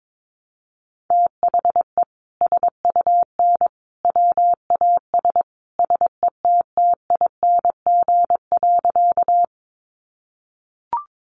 Morse code, heard as a keyed tone, about 22 wpm, 700 Hz. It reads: THE SUN WAS SETTING.
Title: Sentences from Top 1300 Words 22wpm